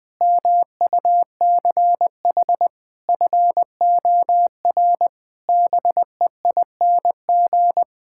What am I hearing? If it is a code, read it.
Morse code, heard as a keyed tone, about 20 wpm, 700 Hz. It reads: MUCH FOR BEING
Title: Sets of 9 Words - 20wpm